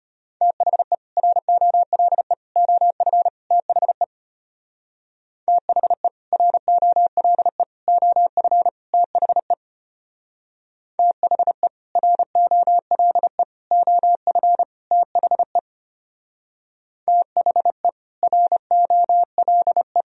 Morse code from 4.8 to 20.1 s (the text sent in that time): THE ROLE OF THE THE ROLE OF THE THE ROLE